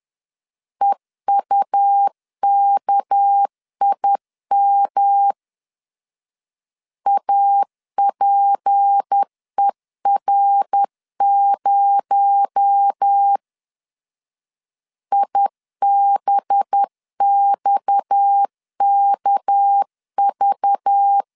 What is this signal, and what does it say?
Morse code, heard as a keyed tone, about 10 wpm, 800 Hz. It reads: EUKIM APER0 IBXKV